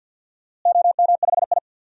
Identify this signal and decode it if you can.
Morse code, heard as a keyed tone, about 53 wpm, 700 Hz. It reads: OM5I
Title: DXCC-100 Call Signs - 4 Characters - Repeated 3x - 50wpm